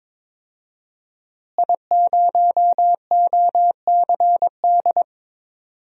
Morse code, heard as a keyed tone, about 22 wpm, 700 Hz. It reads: I0OCD
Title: DXCC-100 Call Signs - 5 Characters - Repeated 2x - 22wpm